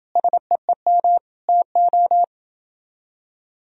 Morse code, heard as a keyed tone, about 28 wpm, 700 Hz. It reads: SEEM TO